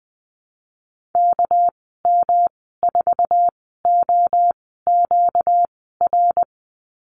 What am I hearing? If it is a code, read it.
Morse code, heard as a keyed tone, about 20 wpm, 700 Hz. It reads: K M 4 O Q R
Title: 238 Lesson 38g - Letter q - Call Signs 20wpm